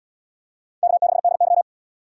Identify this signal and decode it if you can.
Morse code, heard as a keyed tone, about 73 wpm, 700 Hz. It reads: K6DY